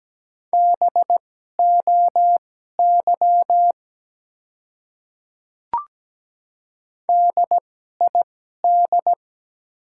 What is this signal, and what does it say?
Morse code, heard as a keyed tone, about 17 wpm, spaced as for 12 wpm, 700 Hz. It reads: BOY DID
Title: 2x Character Spacing - Top 200 Words - 17wpm